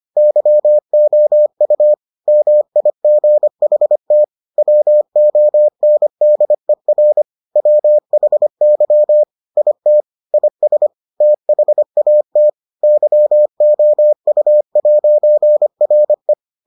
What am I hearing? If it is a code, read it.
Morse code, heard as a keyed tone, about 25 wpm, 600 Hz. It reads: YOU MIGHT WONDER WHY IT IS THAT YOU'RE